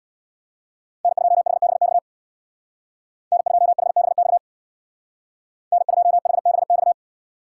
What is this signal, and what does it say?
Morse code, heard as a keyed tone, about 61 wpm, 700 Hz. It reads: N2HBX N2HBX N2HBX
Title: Speed Racing - USA 1x3 Call Signs 50wpm